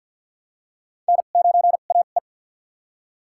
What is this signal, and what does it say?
Morse code, heard as a keyed tone, about 48 wpm, spaced as for 35 wpm, 700 Hz. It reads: N9AE
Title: USA Call Signs - 4 Characters - 2x Character Spacing - 50wpm